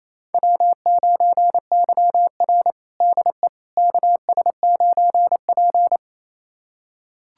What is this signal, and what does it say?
Morse code, heard as a keyed tone, about 28 wpm, 700 Hz. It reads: W9YR DE KS9P
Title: USA Callsign DE Callsign - 4 Characters - 28wpm